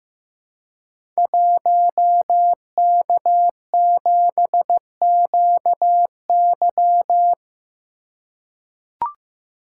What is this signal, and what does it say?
Morse code, heard as a keyed tone, about 15 wpm, 700 Hz. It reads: 1K7QY
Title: Code Groups - 5 Letter:Numbers - 15wpm